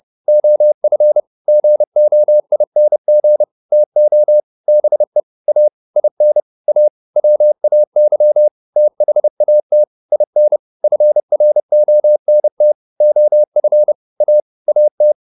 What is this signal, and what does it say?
Morse code, heard as a keyed tone, about 30 wpm, 600 Hz. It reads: OF GOING TO BE A IN A WAY THAT IN FRONT OF A AT